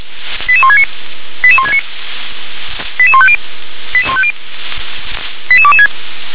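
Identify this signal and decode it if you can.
Five-tone selective call (ZVEI), heard as quick runs of five tones, tones of about 70 ms. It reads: 88169 88169 88150 88150 88226
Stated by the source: ZVEI five-tone selective call 88007, received by SDR.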